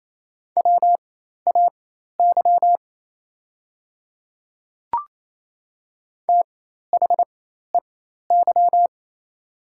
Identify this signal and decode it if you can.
Morse code, heard as a keyed tone, about 28 wpm, spaced as for 13 wpm, 700 Hz. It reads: WAY THEY
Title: 4x Character Spacing - Top 100 Words - 28wpm